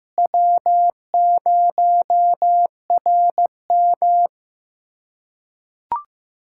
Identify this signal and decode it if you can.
Morse code, heard as a keyed tone, about 15 wpm, 700 Hz. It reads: W0RM
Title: USA Callsign DE Callsign - 4 Characters - 15wpm